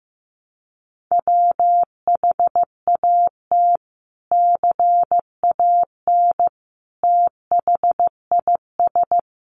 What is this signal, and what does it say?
Morse code, heard as a keyed tone, about 15 wpm, 700 Hz. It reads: WHAT CAN THIS
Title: Sentences from Top 100 Words 15wpm